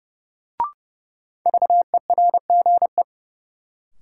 Morse code, heard as a keyed tone, about 32 wpm, 700 Hz. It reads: VERGE
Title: Sets of 4 Words - New Word Drill - 30wpm